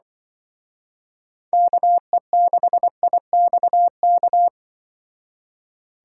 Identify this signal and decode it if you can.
Morse code, heard as a keyed tone, about 24 wpm, 700 Hz. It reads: KE6IXK
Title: Speed Racing - USA 2x3 Call Signs 20wpm